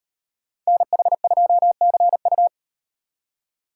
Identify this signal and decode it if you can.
Morse code, heard as a keyed tone, about 36 wpm, 700 Hz. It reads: NH2CU